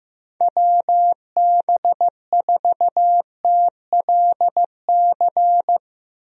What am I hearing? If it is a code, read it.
Morse code, heard as a keyed tone, about 15 wpm, 700 Hz. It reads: WB4TLC
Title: USA Call Signs - 6 Characters - Repeated 2x - 15wpm